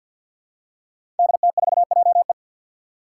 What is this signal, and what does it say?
Morse code, heard as a keyed tone, about 52 wpm, 700 Hz. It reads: DT4JE
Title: Five Challenging Letter-Number Combinations 50wpm